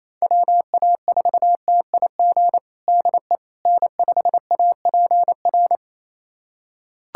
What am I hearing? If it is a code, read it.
Morse code, heard as a keyed tone, about 28 wpm, 700 Hz. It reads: WA4TIG DE N5APR
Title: USA Callsign DE Callsign 28wpm